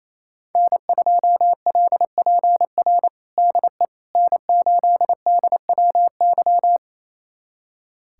Morse code, heard as a keyed tone, about 28 wpm, 700 Hz. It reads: N2LPR DE N8DWY
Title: USA Callsign DE Callsign - 5 Characters - 28wpm